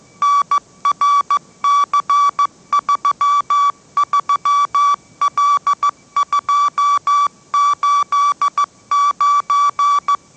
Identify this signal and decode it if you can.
Morse code, heard as a keyed tone, about 15 wpm, 1150 Hz. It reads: NRC33L289
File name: mrx / mrx065 16wpm